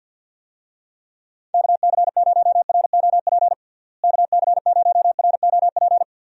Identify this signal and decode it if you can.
Morse code, heard as a keyed tone, about 51 wpm, 700 Hz. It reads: KC0ROP KC0ROP
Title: USA Call Signs - 6 Characters - Repeated 2x - 50wpm